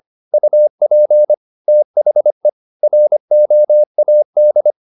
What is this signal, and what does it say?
Morse code, heard as a keyed tone, about 25 wpm, 600 Hz. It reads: UP THE ROAD
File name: the old man and the sea part 1 25wpm